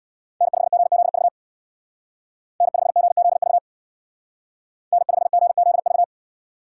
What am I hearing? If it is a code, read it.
Morse code, heard as a keyed tone, about 62 wpm, 700 Hz. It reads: N5GZV N5GZV N5GZV
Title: Speed Racing - USA Call Signs 50wpm